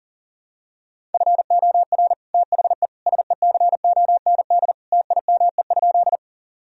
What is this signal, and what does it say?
Morse code, heard as a keyed tone, about 40 wpm, 700 Hz. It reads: FOR THE SECOND TIME?
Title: Sentences from Top 1000 Words - Review Part 1 - 40wpm